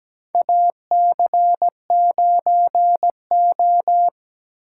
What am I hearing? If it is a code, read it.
Morse code, heard as a keyed tone, about 17 wpm, 700 Hz. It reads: AC9O
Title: USA Call Signs - 8x Character Spacing - 17wpm